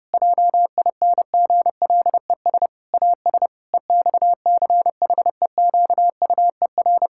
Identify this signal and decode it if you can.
Morse code, heard as a keyed tone, about 30 wpm, 700 Hz. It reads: JINGLES AS EXCHEQUER
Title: Pangram Challenge v2 - 30wpm